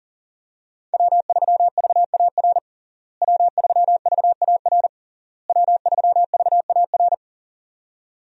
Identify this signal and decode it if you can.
Morse code, heard as a keyed tone, about 40 wpm, 700 Hz. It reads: W3VAR W3VAR W3VAR
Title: USA Call Signs - 5 Characters - Repeated 3x - 40wpm